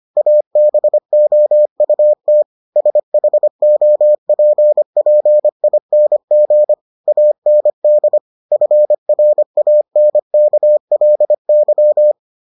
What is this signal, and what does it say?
Morse code, heard as a keyed tone, about 25 wpm, 600 Hz. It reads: ABOUT SHOPPING AND FRANKLY